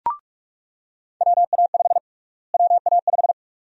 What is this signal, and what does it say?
Morse code, heard as a keyed tone, about 42 wpm, 700 Hz. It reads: WA5 WA5
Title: USA Call Sign Prefixes - Repeated 2x - 45wpm